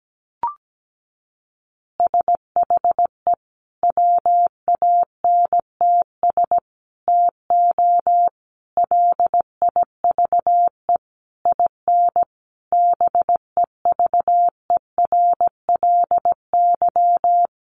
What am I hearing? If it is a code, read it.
Morse code, heard as a keyed tone, about 17 wpm, 700 Hz. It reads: SHE WANTS TO LIVE IN BEVERLY
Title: Easy Phrases 17wpm